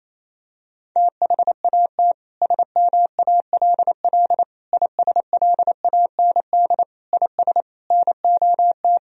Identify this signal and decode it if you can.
Morse code, heard as a keyed tone, about 28 wpm, 700 Hz. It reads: THAT SMALL ISLAND IS NOT
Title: Sentences from Top 1000 Words - Review Part 4 - 28wpm